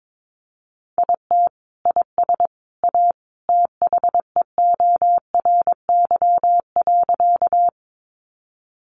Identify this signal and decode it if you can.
Morse code, heard as a keyed tone, about 22 wpm, 700 Hz. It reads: IT IS A THEORY.
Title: Sentences from Top 600 Words 22wpm